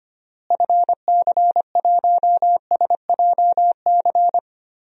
Morse code, heard as a keyed tone, about 25 wpm, 700 Hz. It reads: FC1SJC
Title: DXCC-100 Call Signs - 6 Characters - Repeated 3x - 25wpm